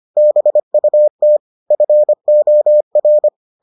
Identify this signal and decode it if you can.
Morse code, heard as a keyed tone, about 25 wpm, 600 Hz. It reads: BUT FOR